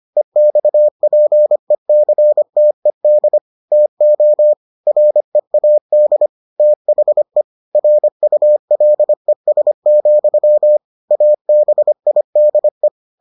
Morse code, heard as a keyed tone, about 25 wpm, 600 Hz. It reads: EXPECTED TO READ THE RULES, ABIDE